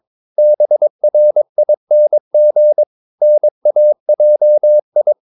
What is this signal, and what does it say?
Morse code, heard as a keyed tone, about 22 wpm, 600 Hz. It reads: BRING NAJI
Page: Romanen Never af Ken Follet på engelsk.